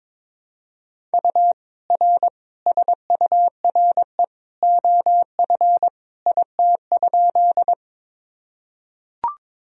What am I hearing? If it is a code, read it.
Morse code, heard as a keyed tone, about 22 wpm, 700 Hz. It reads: U R SURE OF IT?
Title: Sentences from Top 300 Words and Abbreviations 22wpm